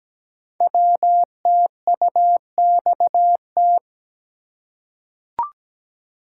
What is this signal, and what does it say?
Morse code, heard as a keyed tone, about 17 wpm, 700 Hz. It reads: WTUXT
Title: Code Groups - 5 Letters - 17wpm